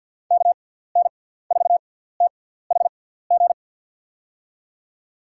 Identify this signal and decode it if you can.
Morse code, heard as a keyed tone, about 50 wpm, spaced as for 17 wpm, 700 Hz. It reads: KN4THG
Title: USA Call Signs - 6x Character Spacing - 50wpm